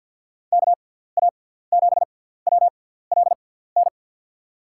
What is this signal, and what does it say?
Morse code, heard as a keyed tone, about 48 wpm, spaced as for 17 wpm, 700 Hz. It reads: KA7WLN
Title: USA Call Signs - 6x Character Spacing - 50wpm